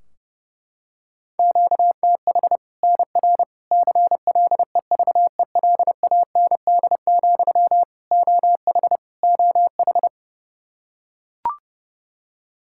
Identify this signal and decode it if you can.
Morse code, heard as a keyed tone, about 30 wpm, 700 Hz. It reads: QTH NR CLEVELAND, OH OH
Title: Practice QSO - Exchange 1 - Beginning - 30wpm